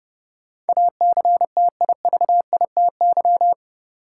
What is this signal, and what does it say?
Morse code, heard as a keyed tone, about 30 wpm, 700 Hz. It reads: ACTIVITY
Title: Sentences from Top 1000 Words - Review Part 4 - New Word Drill - 30wpm